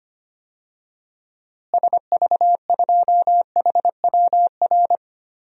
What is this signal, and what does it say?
Morse code, heard as a keyed tone, about 25 wpm, 700 Hz. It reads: SV2HWR
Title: DXCC-100 Call Signs - 6 Characters - Repeated 2x - 25wpm